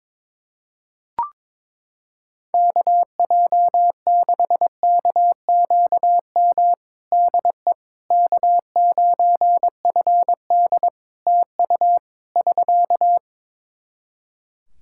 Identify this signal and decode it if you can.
Morse code, heard as a keyed tone, about 22 wpm, 700 Hz. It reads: KJ6KQM DE K9FD TU <SK>
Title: K1USN SST QSO Practice 22wpm